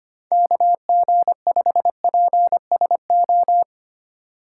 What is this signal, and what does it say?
Morse code, heard as a keyed tone, about 25 wpm, 700 Hz. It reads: KG5PSO
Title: USA Call Signs - 6 Characters - 2x Character Spacing - 25wpm